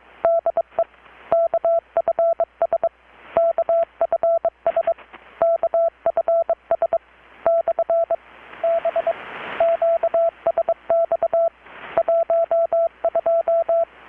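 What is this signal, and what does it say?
Morse code, heard as a keyed tone, about 23 wpm, 670 Hz. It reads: DE KFS KFS KFS / B QSX 12
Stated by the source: Morse from KPH's Night of Nights XX maritime radiotelegraphy event, received in Chiapas, Mexico about 3,800 km away on a Kenwood R-600 with a grounded 10 m long-wire antenna.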